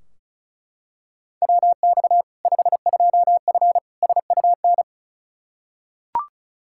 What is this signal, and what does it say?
Morse code, heard as a keyed tone, about 36 wpm, 700 Hz. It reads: WX 52F SUN
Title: Practice QSO - Exchange 2 - Info - 35wpm